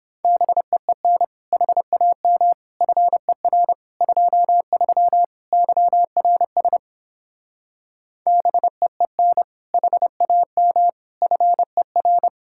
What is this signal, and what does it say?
Morse code, heard as a keyed tone, about 29 wpm, 700 Hz. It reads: BEEN HAM FER 23 YRS BEEN HAM FER